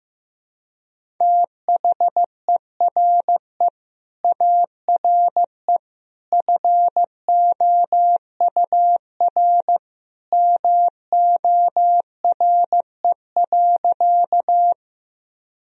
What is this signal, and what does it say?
Morse code, heard as a keyed tone, about 15 wpm, 700 Hz. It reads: THERE ARE FOUR MORE.
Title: Sentences from Top 1000 Words - Review Part 1 - 15wpm